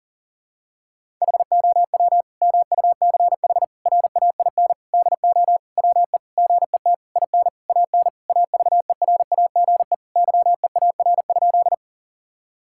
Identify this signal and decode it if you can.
Morse code, heard as a keyed tone, about 40 wpm, 700 Hz. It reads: HOW MUCH RAIN DO WE GET IN AN AVERAGE YEAR?
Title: Sentences from Top 1000 Words - Review Part 6 - 40wpm